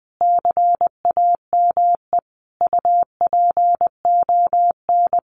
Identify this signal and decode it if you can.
Morse code, heard as a keyed tone, about 20 wpm, 700 Hz. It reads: CAME UPON
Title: Sentences from Top 1400 Words 20wpm